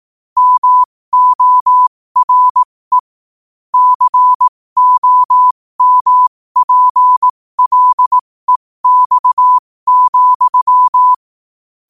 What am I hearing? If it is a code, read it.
Morse code, heard as a keyed tone, about 18 wpm, spaced as for 15 wpm, 1000 Hz. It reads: MORE COMPLEX,
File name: qotdfiles / 15wpm